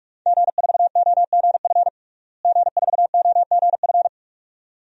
Morse code, heard as a keyed tone, about 46 wpm, 700 Hz. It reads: G4OGF G4OGF
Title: DXCC-100 Call Signs - 5 Characters - Repeated 2x - 45wpm